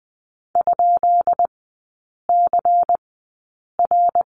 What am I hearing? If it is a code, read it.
Morse code, heard as a keyed tone, about 20 wpm, spaced as for 8 wpm, 690 Hz. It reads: ?CR
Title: 133 Lesson 22e - Question Mark - Three Characters 20wpm